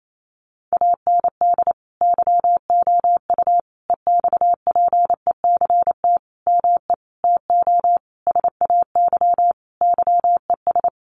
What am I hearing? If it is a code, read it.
Morse code, heard as a keyed tone, about 28 wpm, 700 Hz. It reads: AND YOU EXPECT ME TO SAY YES
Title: Sentences from Top 1100 Words 28wpm